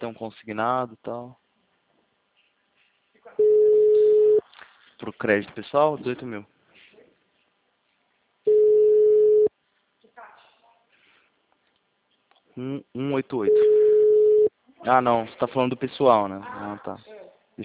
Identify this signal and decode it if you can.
Ringback tone (Europe, 425 Hz), with speech.